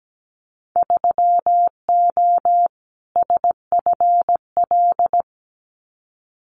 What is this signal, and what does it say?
Morse code, heard as a keyed tone, about 17 wpm, 700 Hz. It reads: 3O SFL
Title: Winter Field Day Exchanges 17wpm